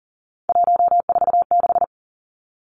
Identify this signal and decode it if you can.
Morse code, heard as a keyed tone, about 40 wpm, 700 Hz. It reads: 146
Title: Numbers - 3-Digit 40wpm